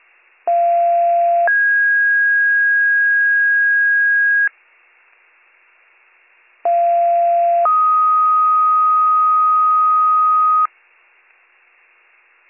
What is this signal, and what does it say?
Two-tone page, heard as a one-second tone, then a three-second tone, about 690 then 1650 Hz, about 690 then 1250 Hz. It is heard from 0.5 to 10.7 s.